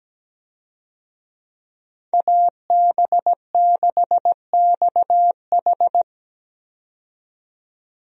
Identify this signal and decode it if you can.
Morse code, heard as a keyed tone, about 17 wpm, 700 Hz. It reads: AB6XH
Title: USA Call Signs - 5 Characters - 17wpm